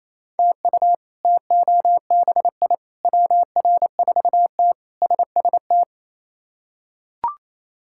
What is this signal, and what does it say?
Morse code, heard as a keyed tone, about 28 wpm, 700 Hz. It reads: TU TOBI WR4T SST
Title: K1USN SST QSO Practice - 28wpm